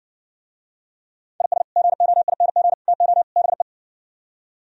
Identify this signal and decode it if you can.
Morse code, heard as a keyed tone, about 68 wpm, 700 Hz. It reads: IS GOING TO BE